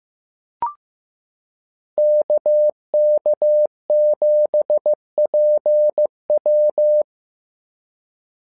Morse code, heard as a keyed tone, about 15 wpm, 600 Hz. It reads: KK7PW
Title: SOTA QSO Practice 15wpm